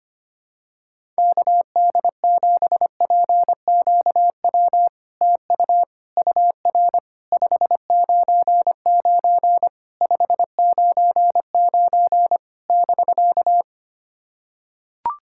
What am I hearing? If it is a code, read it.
Morse code, heard as a keyed tone, about 25 wpm, 690 Hz. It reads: KD7PQW TU UR 599 599 <BK>
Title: SOTA QSO Practice 25wpm